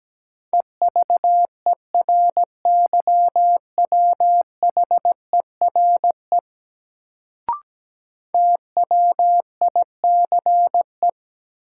Morse code, heard as a keyed tone, about 17 wpm, 700 Hz. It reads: EVERYWHERE TWICE
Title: Sentences from Top 1600 Words - New Word Drill Alt - 17wpm